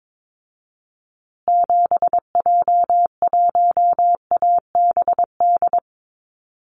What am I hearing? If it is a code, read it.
Morse code, heard as a keyed tone, about 22 wpm, 690 Hz. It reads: 7J1ABD
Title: Call Signs Repeated 3x - 22wpm